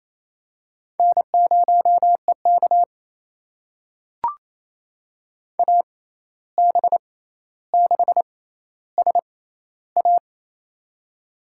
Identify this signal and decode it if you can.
Morse code, heard as a keyed tone, about 28 wpm, spaced as for 13 wpm, 700 Hz. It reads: N0EK AB6SA
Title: USA Call Signs - 6x Character Spacing - 28wpm